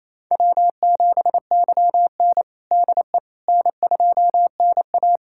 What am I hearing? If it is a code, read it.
Morse code, heard as a keyed tone, about 28 wpm, 700 Hz. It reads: W7YN DE N2NA
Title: USA Callsign DE Callsign - 4 Characters - 28wpm